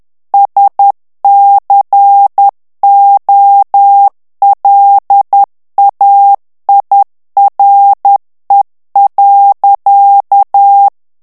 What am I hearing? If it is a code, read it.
Morse code, heard as a keyed tone, about 10 wpm, 800 Hz. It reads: SCOLAIRE.
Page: Français - texte en clair 1 - vitesse : 10 mots minute